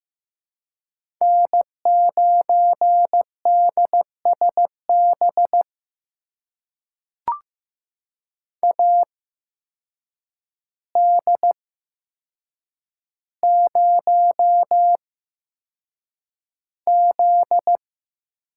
Morse code, heard as a keyed tone, about 15 wpm, spaced as for 7 wpm, 700 Hz. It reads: N9DSB AD0Z
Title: USA Call Signs - 5 Characters - 8x Character Spacing - 15wpm